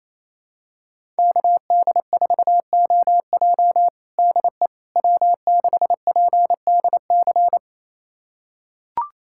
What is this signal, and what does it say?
Morse code, heard as a keyed tone, about 27 wpm, 700 Hz. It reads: KD4OJ DE W6PDC
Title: USA Callsign DE Callsign - 5 Characters - 28wpm